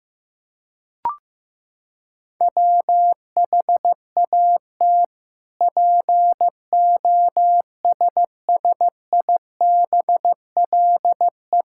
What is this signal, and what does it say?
Morse code, heard as a keyed tone, about 15 wpm, 700 Hz. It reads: WHAT POSSIBLE
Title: Sentences from Top 1000 Words - Review Part 5 - 15wpm